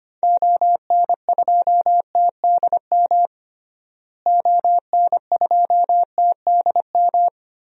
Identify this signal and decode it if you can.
Morse code, heard as a keyed tone, about 25 wpm, 700 Hz. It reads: ON2TDM ON2TDM
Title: DXCC-100 Call Signs - 6 Characters - Repeated 2x - 25wpm